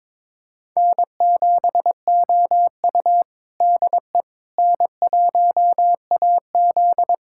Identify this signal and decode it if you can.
Morse code, heard as a keyed tone, about 22 wpm, 700 Hz. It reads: N7OU DE N1AZ
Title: USA Callsign DE Callsign - 4 Characters - 22wpm